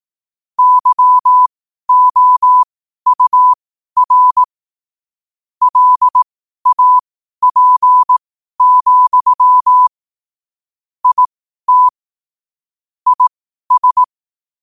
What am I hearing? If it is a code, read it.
Morse code, heard as a keyed tone, about 18 wpm, spaced as for 12 wpm, 1000 Hz. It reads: YOUR LAP, IT IS
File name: qotdfiles / 12wpm